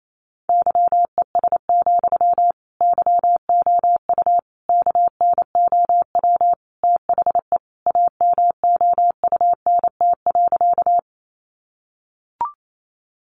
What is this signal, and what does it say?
Morse code, heard as a keyed tone, about 28 wpm, 700 Hz. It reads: YES, YOU KNOW THE AMOUNT.
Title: Sentences from Top 700 Words 28wpm